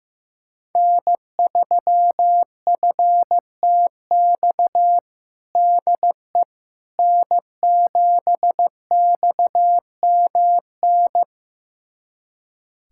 Morse code, heard as a keyed tone, about 15 wpm, 700 Hz. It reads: N3FTX DE N7XMN
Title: USA Callsign DE Callsign - 5 Characters - 15wpm